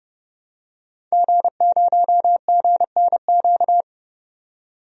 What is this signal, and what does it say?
Morse code, heard as a keyed tone, about 30 wpm, 700 Hz. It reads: G0GNQ
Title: DXCC-100 Call Signs - 5 Characters - 30wpm